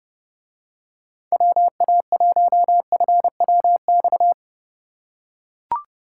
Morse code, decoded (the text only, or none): WA1FWX